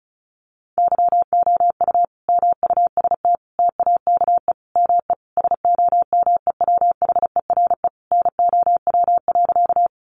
Morse code, heard as a keyed tone, about 34 wpm, 700 Hz. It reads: YOU MUST TAKE ME SOMEWHERE NOW.
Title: Sentences from Top 1400 Words 35wpm